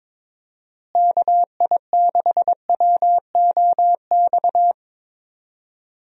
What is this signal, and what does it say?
Morse code, heard as a keyed tone, about 22 wpm, 700 Hz. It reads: KI6WOX
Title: USA Call Signs - 6 Characters - Repeated 2x - 22wpm